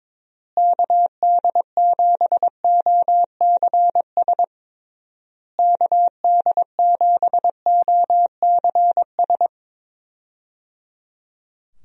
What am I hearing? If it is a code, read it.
Morse code, heard as a keyed tone, about 22 wpm, 700 Hz. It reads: KD7OCS KD7OCS
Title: USA Call Signs - 6 Characters - Repeated 3x - 22wpm